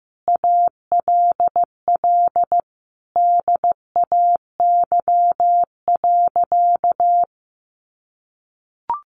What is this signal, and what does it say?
Morse code, heard as a keyed tone, about 15 wpm, 700 Hz. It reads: ALL DAY.
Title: Sentences from Top 1200 Words 15wpm